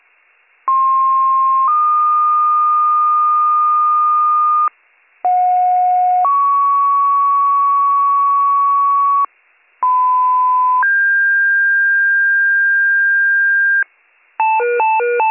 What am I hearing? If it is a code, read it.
Two-tone page, heard as a one-second tone, then a three-second tone, about 1060 then 1220 Hz, about 730 then 1120 Hz, about 1010 then 1600 Hz. It is heard from 0.7 to 13.8 s.